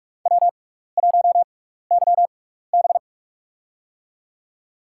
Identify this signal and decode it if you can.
Morse code, heard as a keyed tone, about 46 wpm, spaced as for 16 wpm, 700 Hz. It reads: W1YB
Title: USA Call Signs - 6x Character Spacing - 45wpm